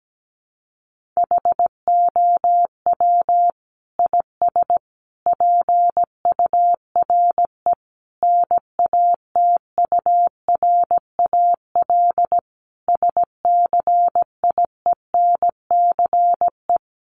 Morse code, heard as a keyed tone, about 17 wpm, 700 Hz. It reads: HOW IS PURE NATURAL SCIENCE
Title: Sentences from Top 1500 Words 17wpm